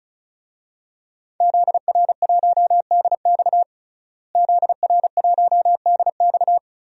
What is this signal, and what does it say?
Morse code, heard as a keyed tone, about 34 wpm, 700 Hz. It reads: ZR1DX ZR1DX
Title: DXCC-100 Call Signs - 5 Characters - Repeated 3x - 35wpm